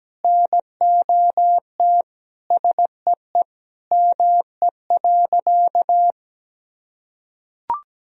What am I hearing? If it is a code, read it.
Morse code, heard as a keyed tone, about 17 wpm, 700 Hz. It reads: NOT SEE ME.
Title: Sentences from Top 1000 Words - Review Part 1 - 17wpm